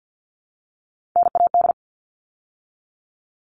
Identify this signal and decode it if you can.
Morse code, heard as a keyed tone, about 48 wpm, 700 Hz. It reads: NAD
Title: All Cut Numbers - 3-Digit 50wpm